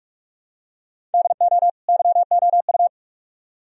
Morse code, heard as a keyed tone, about 46 wpm, 700 Hz. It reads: DO YOU